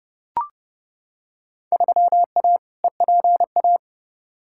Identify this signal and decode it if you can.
Morse code, heard as a keyed tone, about 32 wpm, 700 Hz. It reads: 3A EPA
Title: Field Day Exchanges - 30wpm